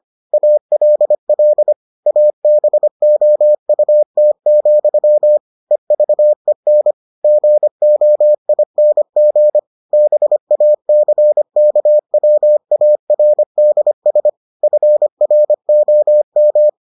Morse code, heard as a keyed tone, about 25 wpm, 600 Hz. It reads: ALL ABOUT, EVEN GOING BACKWARDS FROM